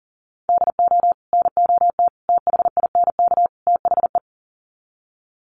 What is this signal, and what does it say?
Morse code, heard as a keyed tone, about 40 wpm, 700 Hz. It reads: DO NOT THINK THE
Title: Sets of 4 Words 40wpm